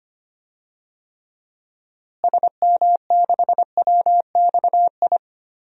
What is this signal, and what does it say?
Morse code, heard as a keyed tone, about 25 wpm, 700 Hz. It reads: SM6WXI
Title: DXCC-100 Call Signs - 6 Characters - Repeated 2x - 25wpm